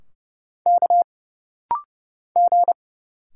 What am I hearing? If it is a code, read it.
Morse code, heard as a keyed tone, about 31 wpm, 700 Hz. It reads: K G
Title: Single Letters - Rapid-Fire 30wpm